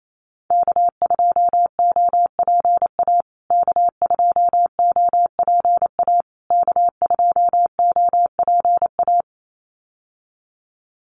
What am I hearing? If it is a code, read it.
Morse code, heard as a keyed tone, about 28 wpm, 700 Hz. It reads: K2OPA K2OPA K2OPA
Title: USA Call Signs Repeated 3x 28wpm